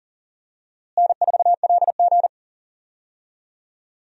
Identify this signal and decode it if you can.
Morse code, heard as a keyed tone, about 40 wpm, 700 Hz. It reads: N4LG
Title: USA Call Signs - 4 Characters - Repeated 3x - 40wpm